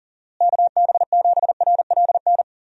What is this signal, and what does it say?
Morse code, heard as a keyed tone, about 40 wpm, 700 Hz. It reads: KB7RLN
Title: USA Call Signs - 6 Characters - Repeated 2x - 40wpm